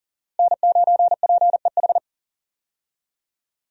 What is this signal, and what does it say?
Morse code, heard as a keyed tone, about 40 wpm, 700 Hz. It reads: N9PEH
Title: USA Call Signs - 5 Characters - Repeated 2x - 40wpm